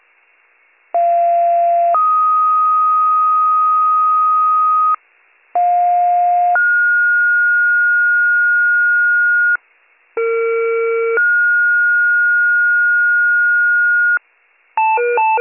Two-tone page, heard as a one-second tone, then a three-second tone, about 690 then 1250 Hz, about 710 then 1430 Hz, about 460 then 1430 Hz, with noise. It is heard from 0.9 to 14.2 s.